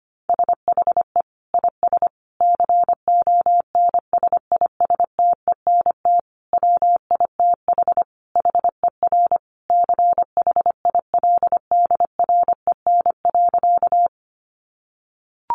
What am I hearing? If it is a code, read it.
Morse code, heard as a keyed tone, about 25 wpm, 700 Hz. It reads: SHE IS CONSISTENT WITH HER CHILDREN.
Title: Sentences from Top 1500 Words 25wpm